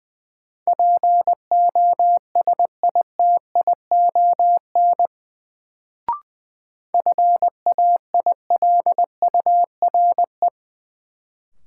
Morse code, heard as a keyed tone, about 20 wpm, 700 Hz. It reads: POSITION FAILURE
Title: Sentences from Top 900 Words - New Word Drill - 20wpm